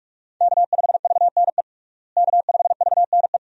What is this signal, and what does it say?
Morse code, heard as a keyed tone, about 46 wpm, 700 Hz. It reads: K5VNE K5VNE
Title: USA Call Signs - 5 Characters - Repeated 2x - 45wpm